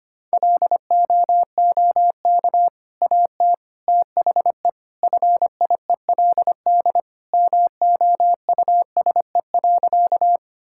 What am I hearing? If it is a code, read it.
Morse code, heard as a keyed tone, about 25 wpm, 700 Hz. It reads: LOOK AT THE FIELD MOUSE.
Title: Sentences from Top 1000 Words - Review Part 5 - 25wpm